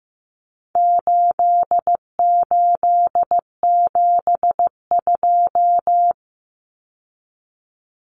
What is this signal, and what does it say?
Morse code, heard as a keyed tone, about 15 wpm, 700 Hz. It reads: 8872
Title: Numbers - 4-Digit 15wpm